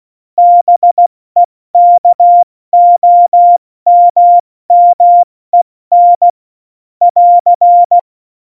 Morse code, heard as a keyed tone, about 16 wpm, 700 Hz. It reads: BEKOMMEN <AR>
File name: Morsekurs / Witze / Witz 16. April 2019, 70 F80